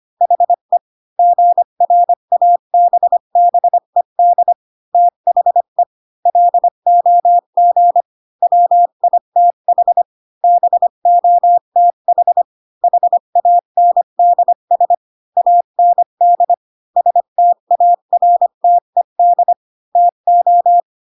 Morse code, heard as a keyed tone, about 25 wpm, 700 Hz. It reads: HE GRABBED THE LOG WITH BOTH HANDS AND STARTED TO
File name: Pinocchio / Pinocchio ch1 20 25 ik1hge